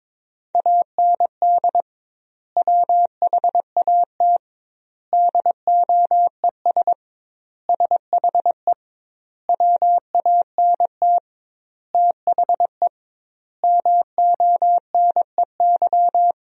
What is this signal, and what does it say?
Morse code, heard as a keyed tone, about 22 wpm, 700 Hz. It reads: AND WHAT DOES SHE WANT THE MONEY
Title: 2x Word-Spacing - Sentences from Top 400 Words - 22wpm